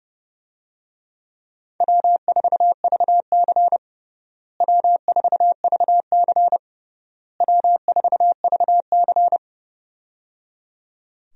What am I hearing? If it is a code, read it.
Morse code, heard as a keyed tone, about 30 wpm, 700 Hz. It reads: W4VC W4VC W4VC
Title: USA Call Signs - 4 Characters - Repeated 3x - 30wpm